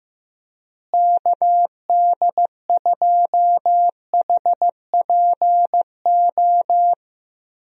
Morse code, heard as a keyed tone, about 15 wpm, 700 Hz. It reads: KD2HPO
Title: USA Call Signs - 6 Characters - Repeated 2x - 15wpm